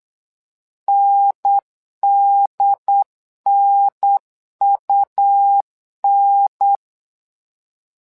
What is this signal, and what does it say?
Morse code, heard as a keyed tone, about 8 wpm, 800 Hz. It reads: NDNUN
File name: LEZIONI / 2 UHDN / 2 UHDN 8wpm2232